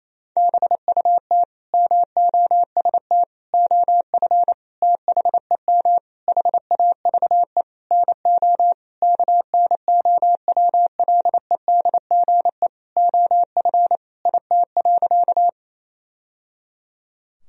Morse code, heard as a keyed tone, about 28 wpm, 700 Hz. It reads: BUT MOST OF THEM HAVE NO KNOWLEDGE OF IT.
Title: Sentences from Top 1000 Words - Review Part 3 - 28wpm